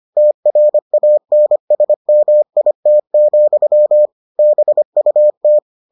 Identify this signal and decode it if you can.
Morse code, heard as a keyed tone, about 25 wpm, 600 Hz. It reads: TRANSMIT, BUT